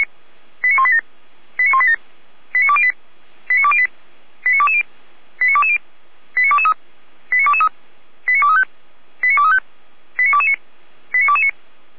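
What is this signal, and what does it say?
Five-tone selective call (ZVEI), heard as quick runs of five tones, tones of about 80 ms. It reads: 89187 89187 89208 89208 89220 89220 89223 89223 89235 89235 89229 89229
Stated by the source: ZVEI five-tone selective call 89240 received on SDR, from the Feuerwehr Orlamünde audio archive.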